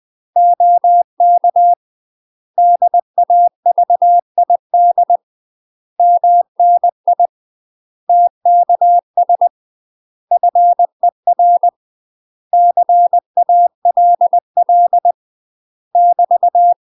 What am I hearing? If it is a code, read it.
Morse code, heard as a keyed tone, about 20 wpm, 700 Hz. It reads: OK DAVID MNI TKS FER CALL <BT>
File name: UK MASTER QSOs TRAINING / 20WPM / UK MASTER QSOs 0005